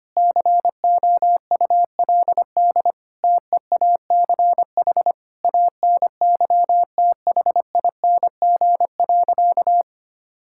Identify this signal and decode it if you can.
Morse code, heard as a keyed tone, about 25 wpm, 700 Hz. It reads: COULD TEACH ANYTHING.
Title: Sentences from Top 1000 Words - Review Part 6 - 25wpm